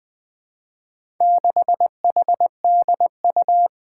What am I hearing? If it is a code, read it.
Morse code, heard as a keyed tone, about 21 wpm, 700 Hz. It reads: 6HDU
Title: Four Challenging Letter-Number Combinations 20wpm